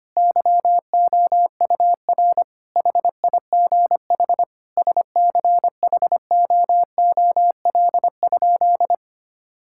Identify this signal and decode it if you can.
Morse code, heard as a keyed tone, about 25 wpm, 700 Hz. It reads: YOUR HIGH SCHOOL?
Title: Sentences from Top 1000 Words - Review Part 6 - 25wpm